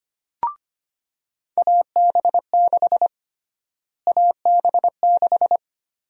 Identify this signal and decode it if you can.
Morse code, heard as a keyed tone, about 25 wpm, 700 Hz. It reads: AB6 AB6
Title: USA Call Sign Prefixes - Repeated 2x - 25wpm